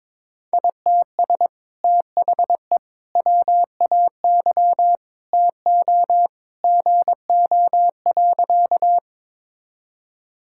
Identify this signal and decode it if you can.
Morse code, heard as a keyed tone, about 21 wpm, 700 Hz. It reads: ITS THE WAY TO GO.
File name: practice / 22wpm / Sentences with 2-Word Phrases - 22wpm